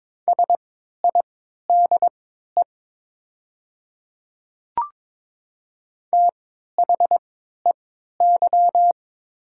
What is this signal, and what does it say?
Morse code, heard as a keyed tone, about 21 wpm, spaced as for 12 wpm, 700 Hz. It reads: SIDE THEY
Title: 3x Character Spacing - Top 100 Words - 22wpm